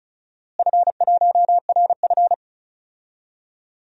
Morse code, heard as a keyed tone, about 36 wpm, 700 Hz. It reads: F1RF
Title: DXCC-100 Call Signs - 4 Characters - Repeated 2x - 35wpm